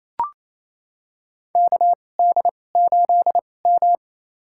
Morse code, heard as a keyed tone, about 28 wpm, spaced as for 20 wpm, 700 Hz. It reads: KD8M
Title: USA Call Signs - 4 Characters - 2x Character Spacing - 28wpm